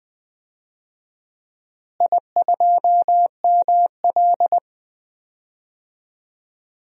Morse code, heard as a keyed tone, about 20 wpm, 700 Hz. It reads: I2ML
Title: DXCC-100 Call Signs - 4 Characters - Repeated 2x - 20wpm